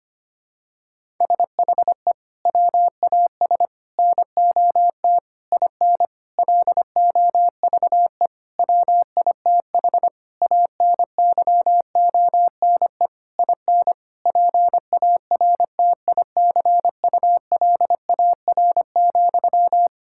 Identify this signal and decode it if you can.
Morse code, heard as a keyed tone, about 25 wpm, 700 Hz. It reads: SHE WAS NOT IN LOVE WITH ANYONE IN PARTICULAR,
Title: Sentences from Top 1000 Words - Review Part 5 - 25wpm